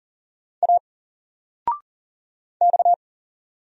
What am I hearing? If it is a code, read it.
Morse code, heard as a keyed tone, about 40 wpm, 700 Hz. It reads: A X
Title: Single Letter-Number Alt 40wpm